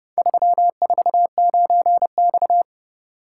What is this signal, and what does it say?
Morse code, heard as a keyed tone, about 31 wpm, 700 Hz. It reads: 349X